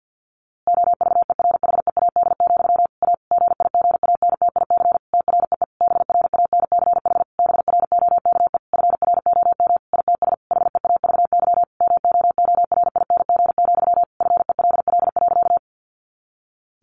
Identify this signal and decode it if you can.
Morse code, heard as a keyed tone, about 48 wpm, 700 Hz. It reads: OVERHEAD, A GIGANTIC TREE BRANCH BROKE FROM ITS HEAVY MOORING, FELL.
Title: A Sound Of Thunder 50wpm